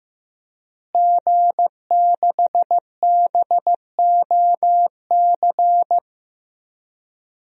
Morse code, heard as a keyed tone, about 15 wpm, 700 Hz. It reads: G6BOC
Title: DXCC-100 Call Signs - 5 Characters - Repeated 3x - 15wpm